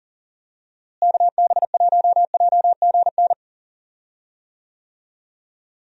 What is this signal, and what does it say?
Morse code, heard as a keyed tone, about 40 wpm, 700 Hz. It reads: KB1JGN
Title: USA Call Signs - 6 Characters - Repeated 3x - 40wpm